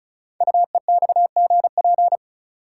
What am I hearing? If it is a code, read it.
Morse code, heard as a keyed tone, about 36 wpm, 700 Hz. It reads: UEXGP
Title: Code Groups - 5 Letters - 35wpm